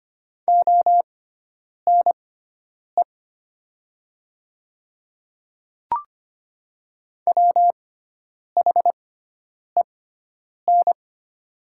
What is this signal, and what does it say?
Morse code, heard as a keyed tone, about 25 wpm, spaced as for 9 wpm, 700 Hz. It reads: ONE WHEN
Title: 6x Character Spacing - Top 100 Words - 25wpm